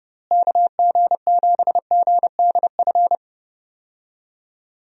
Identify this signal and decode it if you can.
Morse code, heard as a keyed tone, about 30 wpm, 700 Hz. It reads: KG7GDF
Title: USA Call Signs - 6 Characters - Repeated 3x - 30wpm